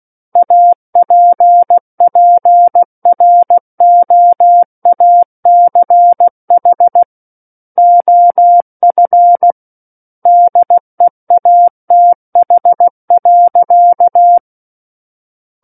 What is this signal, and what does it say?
Morse code, heard as a keyed tone, about 16 wpm, 700 Hz. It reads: APPROACH OF DEATH.
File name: Book One / 07-Chapter 7 - 16 WPM